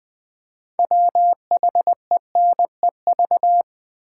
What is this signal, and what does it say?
Morse code, heard as a keyed tone, about 20 wpm, 700 Hz. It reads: WHENEV
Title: Build A Word - 3 Letters At A Time - 8-Letter Words - 20wpm